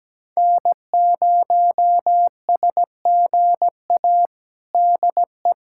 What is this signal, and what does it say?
Morse code, heard as a keyed tone, about 17 wpm, 700 Hz. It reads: N0SGA DE
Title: USA Callsign DE Callsign - 5 Characters - 17wpm